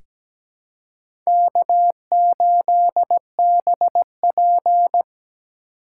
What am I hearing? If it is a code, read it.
Morse code, heard as a keyed tone, about 17 wpm, 700 Hz. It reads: K8BP
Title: USA Call Signs - 4 Characters - 4x Character Spacing - 17wpm